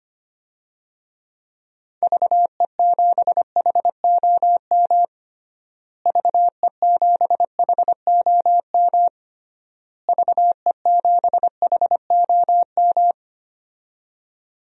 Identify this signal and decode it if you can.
Morse code, heard as a keyed tone, about 25 wpm, 700 Hz. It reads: VE7HOM VE7HOM VE7HOM
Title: DXCC-100 Call Signs - 6 Characters - Repeated 3x - 25wpm